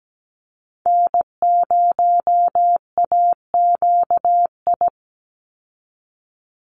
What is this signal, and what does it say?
Morse code, heard as a keyed tone, about 17 wpm, 700 Hz. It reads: N0AQI
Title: USA Call Signs 17wpm